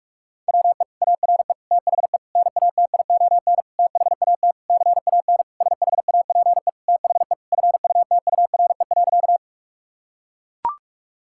Morse code, heard as a keyed tone, about 41 wpm, 700 Hz. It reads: WE ARE THE NATION THAT CAN SHAPE THE FUTURE.
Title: Sentences from Top 1000 Words - Review Part 6 - 45wpm